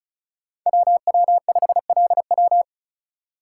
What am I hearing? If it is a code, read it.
Morse code, heard as a keyed tone, about 36 wpm, 700 Hz. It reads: WW5LW
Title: USA Call Signs - 5 Characters - 4x Character Spacing - 35wpm